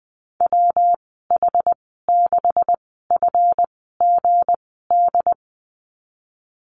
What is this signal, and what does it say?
Morse code, heard as a keyed tone, about 19 wpm, spaced as for 14 wpm, 700 Hz. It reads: WH6FGD
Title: 186 Lesson 30g - Letter b - Call Signs 20wpm